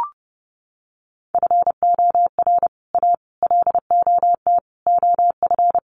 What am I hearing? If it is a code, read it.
Morse code, heard as a keyed tone, about 30 wpm, 700 Hz. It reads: FOR A LOT OF
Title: Sets of 5 Words Encore 30wpm